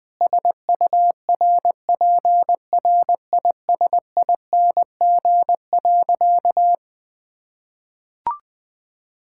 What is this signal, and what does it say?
Morse code, heard as a keyed tone, about 20 wpm, 700 Hz. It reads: SURPRISING.
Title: Sentences with 4-Word Phrases - 20wpm